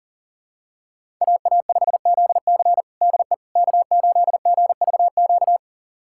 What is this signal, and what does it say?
Morse code, heard as a keyed tone, about 40 wpm, 700 Hz. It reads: AA5ZC DE K8GVQ
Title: USA Callsign DE Callsign - 5 Characters - 40wpm